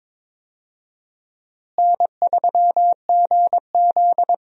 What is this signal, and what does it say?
Morse code, heard as a keyed tone, about 22 wpm, 700 Hz. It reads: N3GZ
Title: USA Call Signs - 4 Characters - Repeated 3x - 22wpm